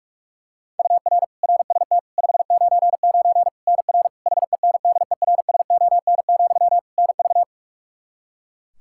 Morse code, heard as a keyed tone, about 46 wpm, 700 Hz. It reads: UR RST 599 NR HENDERSON, NV
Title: Practice QSO - Exchange 1 - Beginning - 45wpm